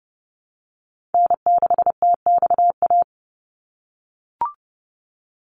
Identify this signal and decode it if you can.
Morse code, heard as a keyed tone, about 30 wpm, 700 Hz. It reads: N6TXA
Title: USA Call Signs 30wpm